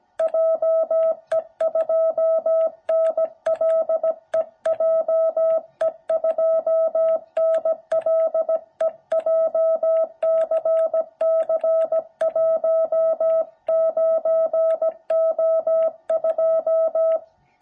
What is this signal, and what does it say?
Morse code, heard as a keyed tone, about 17 wpm, 660 Hz. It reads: JE2NLE JE2NLE JCC19O2
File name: jh8cbh / 2021cw